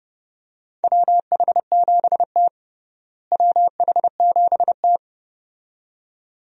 Morse code, heard as a keyed tone, about 30 wpm, 700 Hz. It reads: WH7T WH7T
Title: USA Call Signs - 4 Characters - Repeated 2x - 30wpm